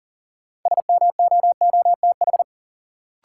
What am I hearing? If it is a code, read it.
Morse code, heard as a keyed tone, about 40 wpm, 700 Hz. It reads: SMOOTH
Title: Sentences from Top 1400 Words - New Word Drill Alt - 40wpm